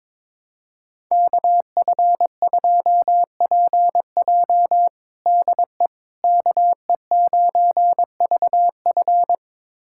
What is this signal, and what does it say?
Morse code, heard as a keyed tone, about 22 wpm, 700 Hz. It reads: KF2PJ DE KE9VF
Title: USA Callsign DE Callsign - 5 Characters - 22wpm